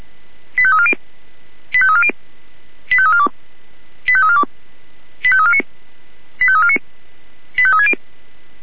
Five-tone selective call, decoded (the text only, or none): 85340 85340 85342 85342 85349 85349 85360